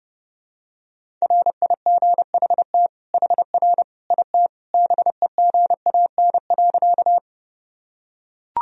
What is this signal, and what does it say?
Morse code, heard as a keyed tone, about 30 wpm, 700 Hz. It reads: RIGHT HR IT BEGAN.
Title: Sentences from Top 300 Words and Abbreviations 30wpm